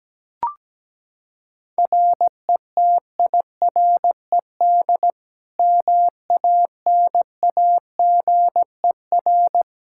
Morse code, heard as a keyed tone, about 17 wpm, 700 Hz. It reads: RETIRED MANAGER
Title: Practice QSO - Exchange 3 - More Info - 17wpm